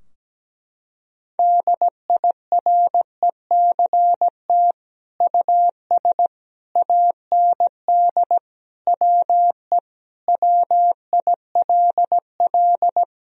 Morse code, heard as a keyed tone, about 17 wpm, 700 Hz. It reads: DIRECT US AND WE WILL
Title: Sentences from Top 1000 Words - Review Part 2 - 17wpm